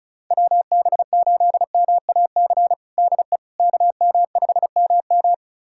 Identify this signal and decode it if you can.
Morse code, heard as a keyed tone, about 36 wpm, 700 Hz. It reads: WB8MAC DE KM5MM
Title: USA Callsign DE Callsign 35wpm